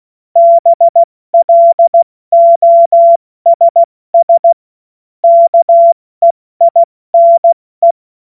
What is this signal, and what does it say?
Morse code, heard as a keyed tone, about 16 wpm, 680 Hz. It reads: BLOSS KEINE
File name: Morsekurs / Witze / Witz 07. September 2021, 70 F80